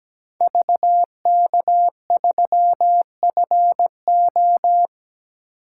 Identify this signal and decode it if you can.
Morse code, heard as a keyed tone, about 20 wpm, 700 Hz. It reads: VK3FO
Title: Speed Racing - USA 2x2 Call Signs 17wpm